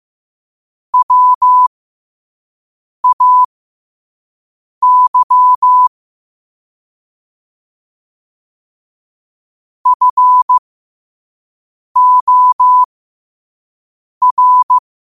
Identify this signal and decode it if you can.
Morse code, heard as a keyed tone, about 15 wpm, spaced as for 5 wpm, 1000 Hz. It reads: WAY FOR
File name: qotdfiles / 5wpm